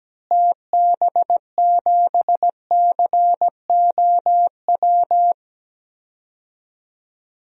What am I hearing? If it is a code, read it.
Morse code, heard as a keyed tone, about 17 wpm, 700 Hz. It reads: TB7COW
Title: DXCC-100 Call Signs - 6 Characters - Repeated 2x - 17wpm